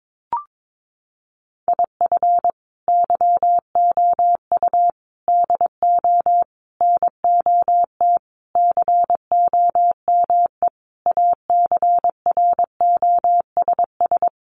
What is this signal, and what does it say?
Morse code, heard as a keyed tone, about 22 wpm, 700 Hz. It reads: IF YOU DO NOT COME ACROSS
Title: Sentences from Top 800 Words 22wpm